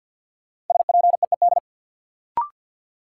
Morse code, heard as a keyed tone, about 50 wpm, 700 Hz. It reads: SPEED
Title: Sentences from Top 800 Words - New Word Drill - 50wpm